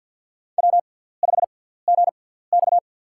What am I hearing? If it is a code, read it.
Morse code, heard as a keyed tone, about 48 wpm, spaced as for 17 wpm, 700 Hz. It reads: W5GX
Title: USA Call Signs - 4 Characters - 6x Character Spacing - 50wpm